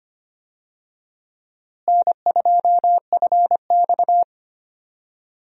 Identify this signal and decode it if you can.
Morse code, heard as a keyed tone, about 25 wpm, 700 Hz. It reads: N2FX
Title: USA Call Signs - 4 Characters - Repeated 2x - 25wpm